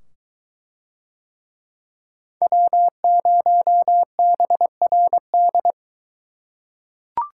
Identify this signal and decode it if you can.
Morse code, heard as a keyed tone, about 22 wpm, 700 Hz. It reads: W0BRD